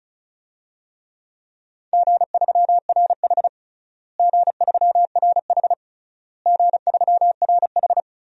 Morse code, heard as a keyed tone, about 36 wpm, 700 Hz. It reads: G3RH G3RH G3RH
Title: DXCC-100 Call Signs - 4 Characters - Repeated 3x - 35wpm